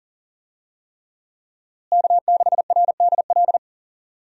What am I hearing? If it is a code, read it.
Morse code, heard as a keyed tone, about 39 wpm, 700 Hz. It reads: K6RDL
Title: USA Call Signs - 5 Characters - Repeated 3x - 40wpm